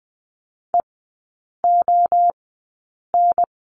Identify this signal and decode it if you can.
Morse code, heard as a keyed tone, about 20 wpm, spaced as for 8 wpm, 700 Hz. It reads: EON
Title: 067 Lesson 12e - Letter d - Three Characters 20wpm